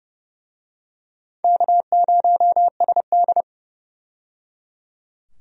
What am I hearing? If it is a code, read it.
Morse code, heard as a keyed tone, about 30 wpm, 700 Hz. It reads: K0SD
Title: USA Call Signs - 4 Characters - 30wpm